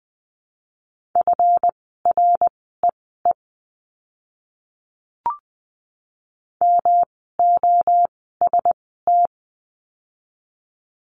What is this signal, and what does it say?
Morse code, heard as a keyed tone, about 20 wpm, spaced as for 14 wpm, 700 Hz. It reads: FREE MOST
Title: 139 Lesson 23f - Letter f - Words 20wpm